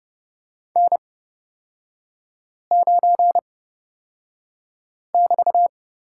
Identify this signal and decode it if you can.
Morse code, heard as a keyed tone, about 30 wpm, spaced as for 5 wpm, 700 Hz. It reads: N9<BT>
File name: Lesson 36e - BT - Three Characters 30wpm